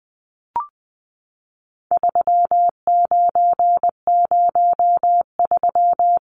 Morse code, heard as a keyed tone, about 20 wpm, 700 Hz. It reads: 3903
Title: Numbers - 4-Digit 20wpm